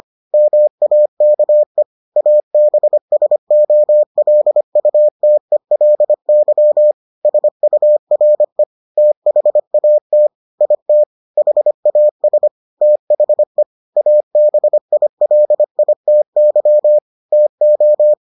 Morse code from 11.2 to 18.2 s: HAS THE ABILITY TO

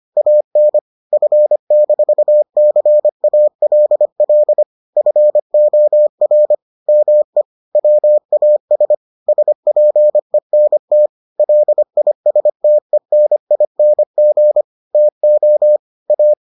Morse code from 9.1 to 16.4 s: SPENT LISTENING TO A

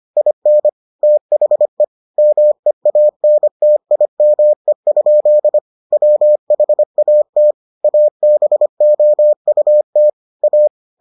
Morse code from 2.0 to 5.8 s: MEANTIME?

